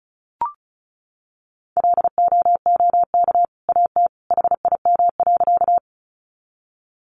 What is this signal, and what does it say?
Morse code, heard as a keyed tone, about 34 wpm, 700 Hz. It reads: LOOK AT HIM.
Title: Sentences from Top 100 Words 35wpm